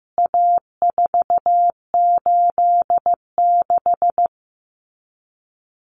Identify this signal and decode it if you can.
Morse code, heard as a keyed tone, about 15 wpm, 700 Hz. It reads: A486
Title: CWT Exchange 15wpm